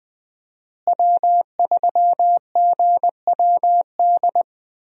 Morse code, heard as a keyed tone, about 20 wpm, 700 Hz. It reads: W3GWD
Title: USA Call Signs - 5 Characters - Repeated 2x - 20wpm